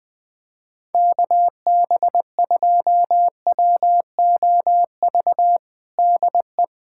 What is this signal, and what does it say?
Morse code, heard as a keyed tone, about 20 wpm, 700 Hz. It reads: KB2WOV DE
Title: USA Callsign DE Callsign - 6 Characters - 20wpm